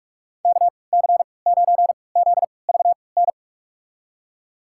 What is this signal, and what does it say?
Morse code, heard as a keyed tone, about 41 wpm, spaced as for 25 wpm, 700 Hz. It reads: KC9ZVN
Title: USA Call Signs - 6 Characters - 3x Character Spacing - 45wpm